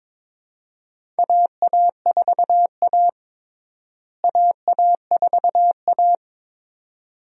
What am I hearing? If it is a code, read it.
Morse code, heard as a keyed tone, about 22 wpm, 700 Hz. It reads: AA4A AA4A
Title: USA Call Signs - 4 Characters - Repeated 2x - 22wpm